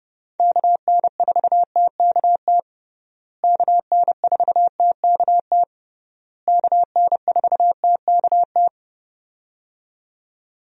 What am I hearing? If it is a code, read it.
Morse code, heard as a keyed tone, about 30 wpm, 700 Hz. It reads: KN4TKT KN4TKT KN4TKT
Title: USA Call Signs - 6 Characters - Repeated 3x - 30wpm